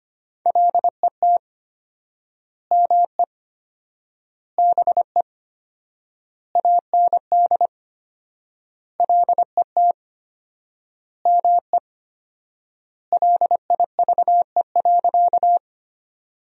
Morse code, heard as a keyed tone, about 25 wpm, 700 Hz. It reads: LET ME BE AND LET ME LIVE.
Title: 4x Word-Spacing - Sentences from Top 300 Words - 25wpm